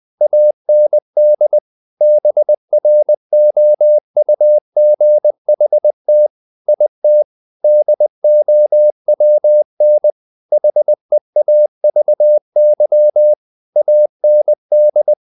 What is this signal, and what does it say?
Morse code, heard as a keyed tone, about 20 wpm, 600 Hz. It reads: AND BROUGHT IT DOWN HEAVY AND
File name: the old man and the sea part 2 20wpm